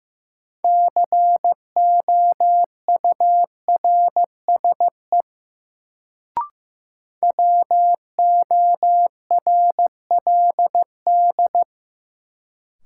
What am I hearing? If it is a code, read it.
Morse code, heard as a keyed tone, about 15 wpm, 700 Hz. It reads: COURSE WORLD
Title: Sets of 2 Words - New Word Drill - 15wpm